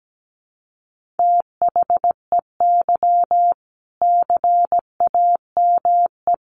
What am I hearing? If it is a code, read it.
Morse code, heard as a keyed tone, about 17 wpm, 700 Hz. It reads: THEY CAME
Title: Sentences from Top 1500 Words 17wpm